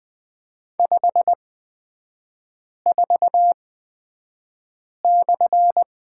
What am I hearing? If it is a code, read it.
Morse code, heard as a keyed tone, about 20 wpm, spaced as for 5 wpm, 700 Hz. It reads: 54/
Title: 255 Lesson 41e - Period - Three Characters 20wpm